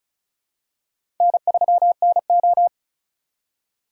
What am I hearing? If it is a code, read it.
Morse code, heard as a keyed tone, about 36 wpm, 700 Hz. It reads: N3NO
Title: USA Call Signs - 4 Characters - 4x Character Spacing - 35wpm